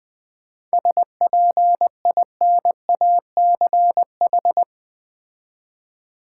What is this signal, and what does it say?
Morse code, heard as a keyed tone, about 20 wpm, 700 Hz. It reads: SPINACH
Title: Difficult Words with Hints - 20wpm